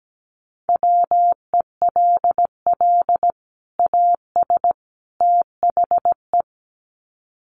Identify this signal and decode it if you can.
Morse code, heard as a keyed tone, about 17 wpm, 700 Hz. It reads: WELL AS THE
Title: Sets of 3 Words 17wpm